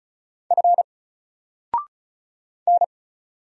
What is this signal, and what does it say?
Morse code, heard as a keyed tone, about 34 wpm, 700 Hz. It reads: F N
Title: Single Letter-Number 35wpm